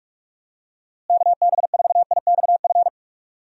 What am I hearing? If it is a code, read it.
Morse code, heard as a keyed tone, about 41 wpm, 700 Hz. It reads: KB4IXF
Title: USA Call Signs - 6 Characters - Repeated 3x - 45wpm